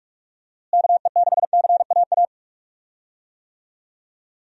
Morse code, heard as a keyed tone, about 46 wpm, 700 Hz. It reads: KE6CAA
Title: USA Call Signs - 6 Characters - Repeated 3x - 45wpm